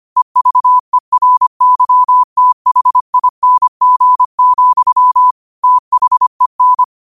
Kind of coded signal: Morse code